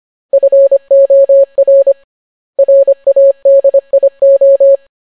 Morse code, heard as a keyed tone, about 25 wpm, 550 Hz. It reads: FOR RADIO